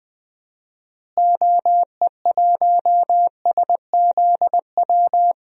Morse code, heard as a keyed tone, about 21 wpm, 700 Hz. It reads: OE1SZW
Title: DXCC-100 Call Signs - 6 Characters - Repeated 3x - 20wpm